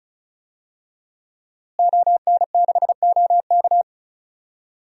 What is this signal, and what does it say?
Morse code, heard as a keyed tone, about 36 wpm, 700 Hz. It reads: ON6OK
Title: DXCC-100 Call Signs - 5 Characters - Repeated 2x - 35wpm